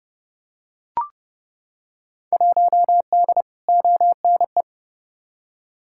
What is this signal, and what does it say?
Morse code, heard as a keyed tone, about 30 wpm, 700 Hz. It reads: 1D ONE
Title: Field Day Exchanges - 30wpm